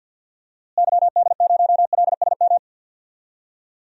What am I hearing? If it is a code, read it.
Morse code, heard as a keyed tone, about 47 wpm, 700 Hz. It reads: YD0LSM
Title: DXCC-100 Call Signs - 6 Characters - Repeated 3x - 50wpm